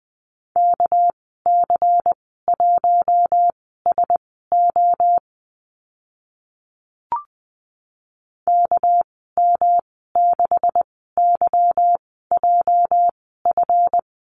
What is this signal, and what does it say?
Morse code, heard as a keyed tone, about 20 wpm, spaced as for 14 wpm, 700 Hz. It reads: KC1SO KM6YJF
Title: 245 Lesson 39g - Letter z - Call Signs 20wpm